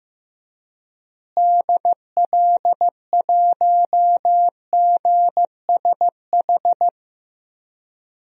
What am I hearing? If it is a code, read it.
Morse code, heard as a keyed tone, about 15 wpm, 700 Hz. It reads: DL1GSH
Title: DXCC-100 Call Signs - 6 Characters - Repeated 2x - 15wpm